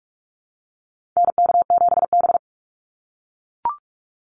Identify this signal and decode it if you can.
Morse code, heard as a keyed tone, about 44 wpm, 700 Hz. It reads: NK7B
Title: Call Signs - 45wpm